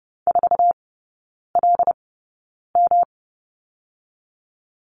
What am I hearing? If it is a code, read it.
Morse code, heard as a keyed tone, about 31 wpm, 700 Hz. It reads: 4 L M
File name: Lesson 19e - Letter w - Three Characters 30wpm